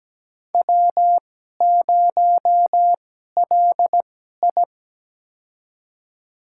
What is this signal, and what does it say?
Morse code, heard as a keyed tone, about 17 wpm, spaced as for 12 wpm, 700 Hz. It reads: W0LI
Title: USA Call Signs - 4 Characters - 2x Character Spacing - 17wpm